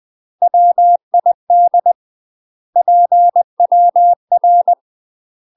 Morse code, heard as a keyed tone, about 20 wpm, 700 Hz. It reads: WID PWR
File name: UK MASTER QSOs TRAINING/20WPM